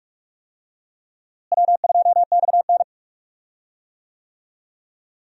Morse code, heard as a keyed tone, about 43 wpm, 700 Hz. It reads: W2XN
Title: USA Call Signs - 4 Characters - Repeated 2x - 45wpm